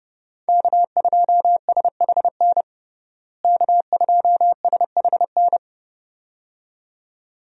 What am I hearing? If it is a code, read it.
Morse code, heard as a keyed tone, about 30 wpm, 700 Hz. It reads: K2SHN K2SHN
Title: USA Call Signs - 5 Characters - Repeated 2x - 30wpm